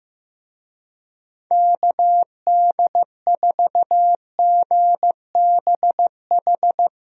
Morse code, heard as a keyed tone, about 15 wpm, 700 Hz. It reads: KD4GBH
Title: USA Call Signs - 6 Characters - Repeated 2x - 15wpm